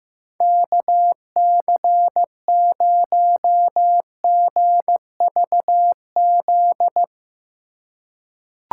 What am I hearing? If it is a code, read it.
Morse code, heard as a keyed tone, about 15 wpm, 700 Hz. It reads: KC0GVZ
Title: USA Call Signs - 3x Character Spacing - 15wpm